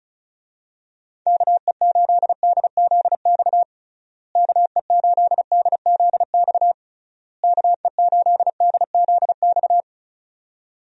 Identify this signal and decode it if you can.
Morse code, heard as a keyed tone, about 34 wpm, 700 Hz. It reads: KE8DZX KE8DZX KE8DZX
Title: USA Call Signs - 6 Characters - Repeated 3x - 35wpm